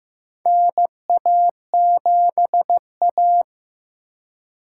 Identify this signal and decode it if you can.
Morse code, heard as a keyed tone, about 15 wpm, 700 Hz. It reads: NA7A
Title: USA Call Signs - 4 Characters - 3x Character Spacing - 15wpm